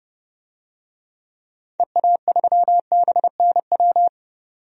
Morse code, heard as a keyed tone, about 30 wpm, 700 Hz. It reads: EA3BNW
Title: DXCC-100 Call Signs - 6 Characters - 30wpm